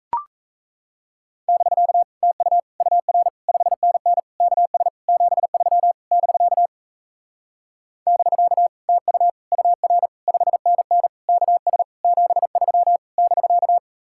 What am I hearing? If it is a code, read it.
Morse code, heard as a keyed tone, about 36 wpm, 700 Hz. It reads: <BK> TU UR 5NN KS 73 <BK> <BK> TU UR 5NN KS 73 <BK>